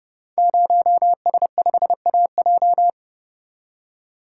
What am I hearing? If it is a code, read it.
Morse code, heard as a keyed tone, about 28 wpm, 700 Hz. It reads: 0S5AJ
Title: Code Groups - 5 Letter:Numbers - 30wpm